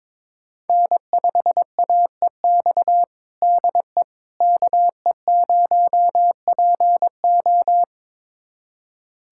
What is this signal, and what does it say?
Morse code, heard as a keyed tone, about 22 wpm, 700 Hz. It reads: N5AEX DE KE0PO
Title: USA Callsign DE Callsign - 5 Characters - 22wpm